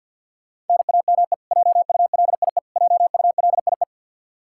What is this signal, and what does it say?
Morse code, heard as a keyed tone, about 52 wpm, 700 Hz. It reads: NAME JULIE JULIE
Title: Practice QSO - Exchange 1 - Beginning - 50wpm